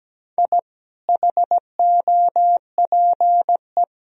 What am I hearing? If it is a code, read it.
Morse code, heard as a keyed tone, about 17 wpm, 700 Hz. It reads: I HOPE
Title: Sentences from Top 1000 Words - Review Part 1 - 17wpm